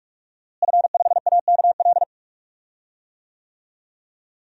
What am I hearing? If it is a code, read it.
Morse code, heard as a keyed tone, about 46 wpm, 700 Hz. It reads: F5AKL
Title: DXCC-100 Call Signs - 5 Characters - Repeated 2x - 45wpm